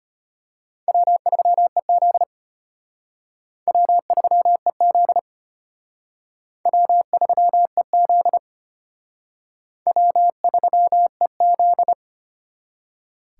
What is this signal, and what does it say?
Morse code, heard as a keyed tone, about 31 wpm, 700 Hz. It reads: W3EZ W3EZ W3EZ W3EZ
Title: Speed Racing - USA Call Signs 25wpm